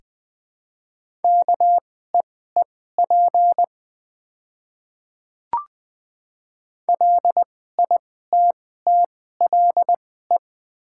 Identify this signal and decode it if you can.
Morse code, heard as a keyed tone, about 20 wpm, spaced as for 14 wpm, 700 Hz. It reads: KEEP LITTLE
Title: 2x Character Spacing - Top 300 Words - 20wpm